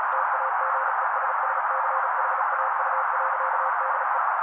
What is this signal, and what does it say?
Morse code, heard as a keyed tone, about 35 wpm, 530 Hz. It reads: TAGESSCHAU.DE
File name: daily / 35wpm noise